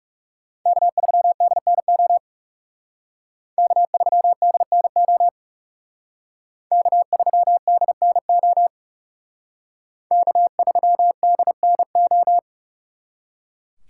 Morse code, heard as a keyed tone, about 36 wpm, 700 Hz. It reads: K3DNO K3DNO K3DNO K3DNO
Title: Speed Racing - USA 1x3 Call Signs 30wpm